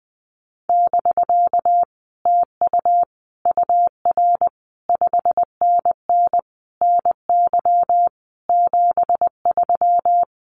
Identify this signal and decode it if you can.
Morse code, heard as a keyed tone, about 19 wpm, 700 Hz. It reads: <BK> TU UR 5NN NY 73
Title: Common QSO Phrases Encore 20wpm